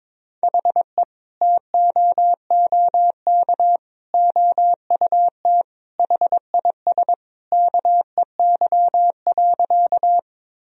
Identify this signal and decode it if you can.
Morse code, heard as a keyed tone, about 22 wpm, 700 Hz. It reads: HE TOOK OUT HIS KEY.
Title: Sentences from Top 1000 Words - Review Part 4 - 22wpm